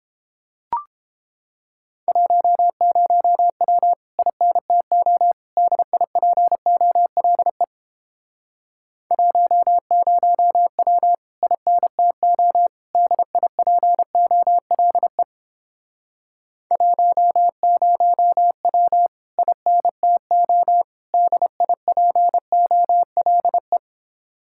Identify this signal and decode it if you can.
Morse code, heard as a keyed tone, about 31 wpm, 700 Hz. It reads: 10W INTO DIPOLE 10W INTO DIPOLE 10W INTO DIPOLE